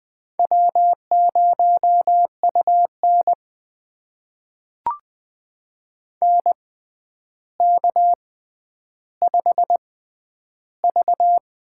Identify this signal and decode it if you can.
Morse code, heard as a keyed tone, about 20 wpm, spaced as for 10 wpm, 700 Hz. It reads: W0UN NK5V
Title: USA Call Signs - 4 Characters - 6x Character Spacing - 20wpm